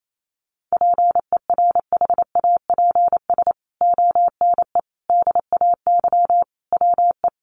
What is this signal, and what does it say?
Morse code, heard as a keyed tone, about 28 wpm, 700 Hz. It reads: PERHAPS ONE DAY WE
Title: Sentences from Top 600 Words 28wpm